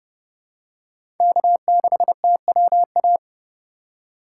Morse code, heard as a keyed tone, about 30 wpm, 700 Hz. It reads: K6TWA
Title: USA Call Signs - 5 Characters - 3x Character Spacing - 30wpm